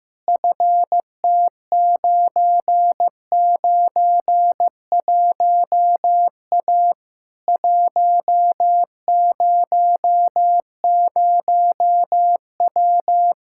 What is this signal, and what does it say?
Morse code, heard as a keyed tone, about 15 wpm, 700 Hz. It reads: FT991A 100W
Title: K1USN SST QSO Practice 15wpm